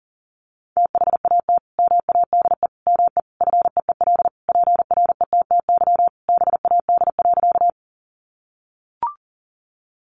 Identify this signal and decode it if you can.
Morse code, heard as a keyed tone, about 40 wpm, 700 Hz. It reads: THAT MADE ME FEEL PRETTY BAD.
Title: Sentences from Top 1000 Words 40wpm